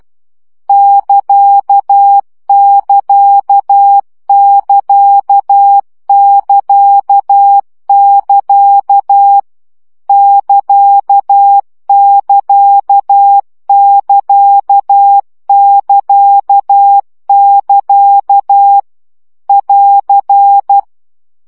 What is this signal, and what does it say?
Morse code, heard as a keyed tone, about 12 wpm, 800 Hz. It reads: <KA><KA><KA><KA><KA> <KA><KA><KA><KA><KA> <AR>
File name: CW gyakorlas ct 10 baud